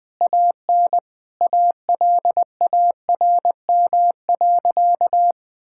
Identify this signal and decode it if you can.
Morse code, heard as a keyed tone, about 20 wpm, 700 Hz. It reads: AN ALARM.
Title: Sentences with 8-Word Phrases - 20wpm